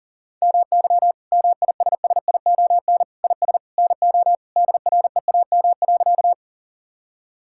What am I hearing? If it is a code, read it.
Morse code, heard as a keyed tone, about 40 wpm, 700 Hz. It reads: MY MISSION IS NO DREAM.
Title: Sentences from Top 1600 Words 40wpm